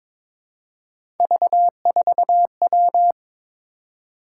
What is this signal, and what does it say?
Morse code, heard as a keyed tone, about 22 wpm, 700 Hz. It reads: V4W
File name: practice / 22wpm / Three Letter-Number Combinations - 22wpm